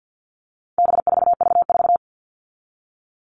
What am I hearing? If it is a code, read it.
Morse code, heard as a keyed tone, about 50 wpm, 700 Hz. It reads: B4V4
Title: All Cut Numbers - 4-Digit 50wpm